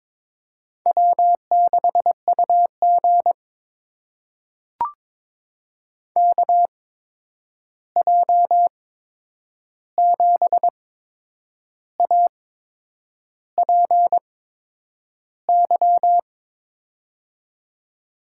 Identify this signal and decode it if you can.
Morse code, heard as a keyed tone, about 22 wpm, spaced as for 8 wpm, 700 Hz. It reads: W6UG KJ7APY
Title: USA Call Signs - 8x Character Spacing - 22wpm